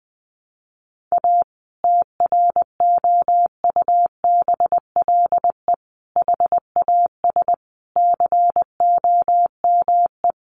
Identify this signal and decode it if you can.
Morse code, heard as a keyed tone, about 20 wpm, 700 Hz. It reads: A TROUBLE HAS COME
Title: Sentences from Top 1200 Words 20wpm